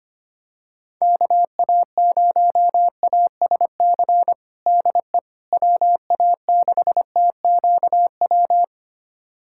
Morse code, heard as a keyed tone, about 25 wpm, 700 Hz. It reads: KA0ASC DE WA6TQW
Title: USA Callsign DE Callsign - 6 Characters - 25wpm